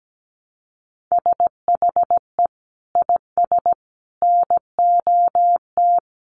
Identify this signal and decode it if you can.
Morse code, heard as a keyed tone, about 17 wpm, 700 Hz. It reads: SHE IS NOT
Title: Sentences from Top 700 Words 17wpm